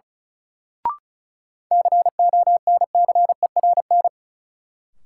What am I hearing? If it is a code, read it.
Morse code, heard as a keyed tone, about 36 wpm, 700 Hz. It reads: CONCERN
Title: Sentences from Top 1000 Words - New Word Drill - 35wpm